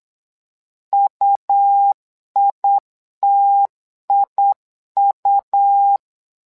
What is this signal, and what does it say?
Morse code, heard as a keyed tone, about 8 wpm, 800 Hz. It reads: UITIU
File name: LEZIONI / 2 UHDN / 2 U 8wpm 2028